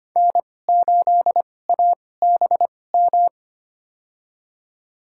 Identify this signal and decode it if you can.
Morse code, heard as a keyed tone, about 25 wpm, spaced as for 18 wpm, 700 Hz. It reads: N8ABM
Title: USA Call Signs - 2x Character Spacing - 25wpm